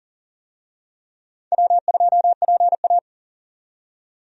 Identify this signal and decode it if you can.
Morse code, heard as a keyed tone, about 40 wpm, 700 Hz. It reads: W2PA
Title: USA Call Signs - 4 Characters - 40wpm